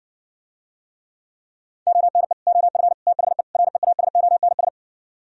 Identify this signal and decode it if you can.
Morse code, heard as a keyed tone, about 68 wpm, 700 Hz. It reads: ONE OF THE REASONS